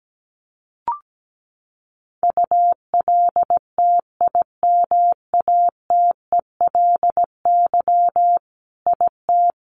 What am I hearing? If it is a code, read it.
Morse code, heard as a keyed tone, about 17 wpm, 700 Hz. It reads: ULTIMATELY IT
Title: Sentences from Top 1200 Words 17wpm